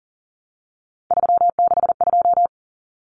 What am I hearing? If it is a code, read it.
Morse code, heard as a keyed tone, about 40 wpm, 700 Hz. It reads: 362
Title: Common Cut Numbers - 3-Digit 40wpm